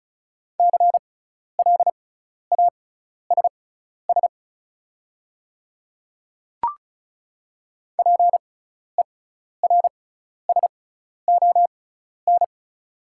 Morse code, heard as a keyed tone, about 34 wpm, spaced as for 12 wpm, 700 Hz. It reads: CLASS PERSON
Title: 6x Character Spacing - Top 500 Words - 35wpm